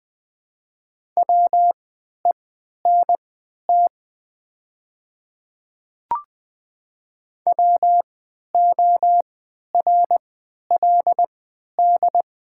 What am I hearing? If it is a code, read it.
Morse code, heard as a keyed tone, about 20 wpm, spaced as for 11 wpm, 700 Hz. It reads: WENT WORLD
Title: 3x Character Spacing - Top 200 Words - 20wpm